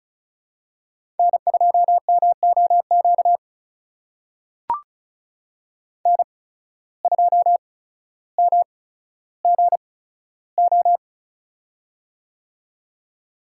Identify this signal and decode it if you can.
Morse code, heard as a keyed tone, about 36 wpm, spaced as for 15 wpm, 700 Hz. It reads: N2MOQ N2MGO
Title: USA Call Signs - 5 Characters - 8x Character Spacing - 35wpm